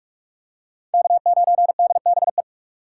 Morse code, heard as a keyed tone, about 43 wpm, 700 Hz. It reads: K9DBE